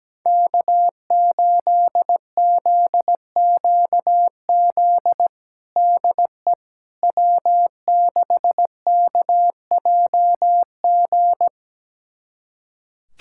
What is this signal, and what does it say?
Morse code, heard as a keyed tone, about 17 wpm, 700 Hz. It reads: K8ZQZ DE W6KJG
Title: USA Callsign DE Callsign - 5 Characters - 17wpm